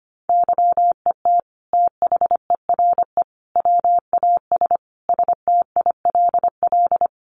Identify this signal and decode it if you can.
Morse code, heard as a keyed tone, about 25 wpm, 700 Hz. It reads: YET THERE WAS STILL
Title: Sentences from Top 1300 Words 25wpm